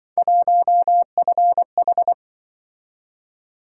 Morse code, heard as a keyed tone, about 25 wpm, 700 Hz. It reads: 1FH